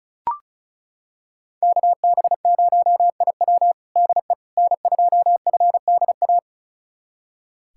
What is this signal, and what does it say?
Morse code, heard as a keyed tone, about 34 wpm, 700 Hz. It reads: KB0IW DE N2FDA
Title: USA Callsign DE Callsign 35wpm